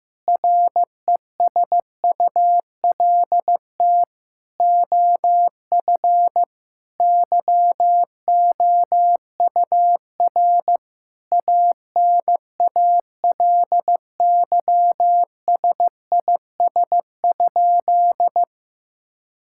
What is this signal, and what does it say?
Morse code, heard as a keyed tone, about 15 wpm, 700 Hz. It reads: RESULT OF YOUR ANALYSIS?
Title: Sentences from Top 1000 Words - Review Part 5 - 15wpm